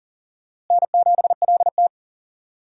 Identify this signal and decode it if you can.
Morse code, heard as a keyed tone, about 40 wpm, 700 Hz. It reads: N7LT
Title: USA Call Signs - 4 Characters - Repeated 2x - 40wpm